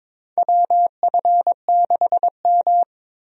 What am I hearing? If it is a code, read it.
Morse code, heard as a keyed tone, about 22 wpm, 700 Hz. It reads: WF6M
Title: USA Call Signs - 4 Characters - Repeated 3x - 22wpm